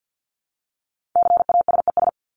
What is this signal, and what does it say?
Morse code, heard as a keyed tone, about 50 wpm, 700 Hz. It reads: CASES
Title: ES:S Suffix - Top 100 Words with Base Words 50wpm